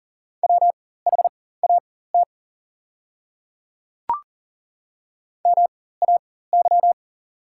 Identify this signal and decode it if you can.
Morse code, heard as a keyed tone, about 40 wpm, spaced as for 19 wpm, 690 Hz. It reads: WHAT MAY
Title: 4x Character Spacing - Top 200 Words - 40wpm